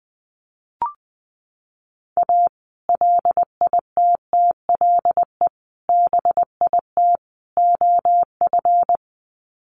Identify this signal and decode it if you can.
Morse code, heard as a keyed tone, about 20 wpm, 700 Hz. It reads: A LITTLE BIT OF
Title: Sets of 4 Words 20wpm